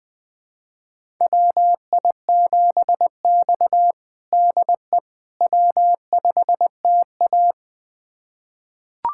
Morse code, heard as a keyed tone, about 20 wpm, 700 Hz. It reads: WI7X DE W5TA
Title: USA Callsign DE Callsign - 4 Characters - 20wpm